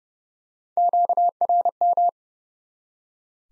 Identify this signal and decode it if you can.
Morse code, heard as a keyed tone, about 30 wpm, 700 Hz. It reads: QRM
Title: Top 90 Abbr 30WPM